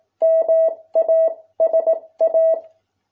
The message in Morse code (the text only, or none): MASA